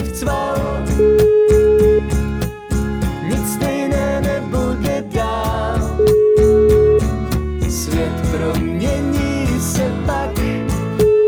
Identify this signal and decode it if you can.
Ringback tone (Europe, 425 Hz), with noise.